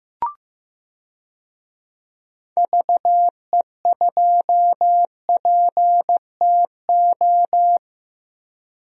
Morse code, heard as a keyed tone, about 15 wpm, 700 Hz. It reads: VE2PTO
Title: DXCC-100 Call Signs - 6 Characters - Repeated 3x - 15wpm